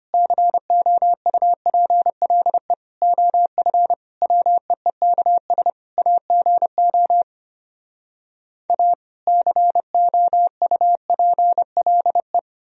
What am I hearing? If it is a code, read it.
Morse code, heard as a keyed tone, about 33 wpm, 700 Hz. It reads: COUPLE OF WEEKS AGO A COUPLE